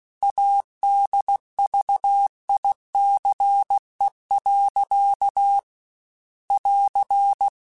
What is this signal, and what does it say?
Morse code, heard as a keyed tone, about 16 wpm, 800 Hz. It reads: ADVICE. <AR>
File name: Quotes / Quotes-mp3s / Advice @15